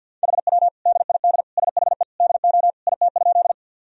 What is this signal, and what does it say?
Morse code, heard as a keyed tone, about 48 wpm, 700 Hz. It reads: HW DID SHE DO IT?
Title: Sentences from Top 100 Words and Abbreviations 50wpm